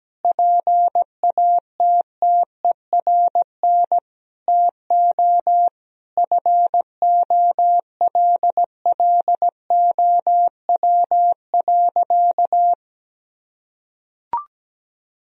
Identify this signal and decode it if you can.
Morse code, heard as a keyed tone, about 17 wpm, 700 Hz. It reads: PATTERN TO FOLLOW.
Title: Sentences from Top 1000 Words - Review Part 2 - 17wpm